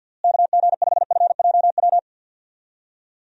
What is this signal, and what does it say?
Morse code, heard as a keyed tone, about 48 wpm, 700 Hz. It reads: KG5FJW
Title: USA Call Signs - 6 Characters - Repeated 2x - 50wpm